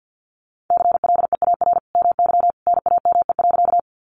Morse code, heard as a keyed tone, about 50 wpm, 700 Hz. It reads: CLEAR MY NAME.
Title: Sentences from Top 500 Words 50wpm